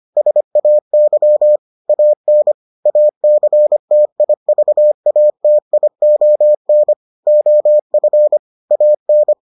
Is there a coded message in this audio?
Yes: Morse code from 0.2 to 9.3 s, heard as a keyed tone, about 25 wpm, 600 Hz. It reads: SAY AN ACTIVATION OF AN